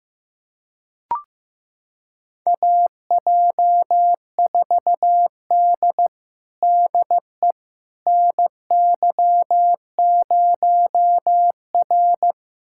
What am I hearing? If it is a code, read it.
Morse code, heard as a keyed tone, about 15 wpm, 700 Hz. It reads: AJ4D DE NY0R
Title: USA Callsign DE Callsign - 4 Characters - 15wpm